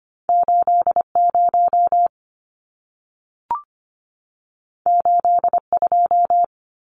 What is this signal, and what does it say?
Morse code, heard as a keyed tone, about 25 wpm, 700 Hz. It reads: 80 82
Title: Numbers - 2-Digit 25wpm